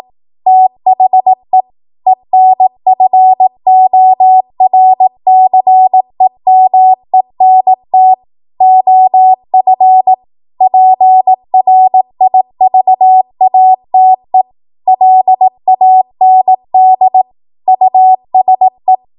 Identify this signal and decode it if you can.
Morse code, heard as a keyed tone, about 18 wpm, 750 Hz. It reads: THE ENFORCEMENT OF PRIVATE LAND USE
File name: Morse / 18WPM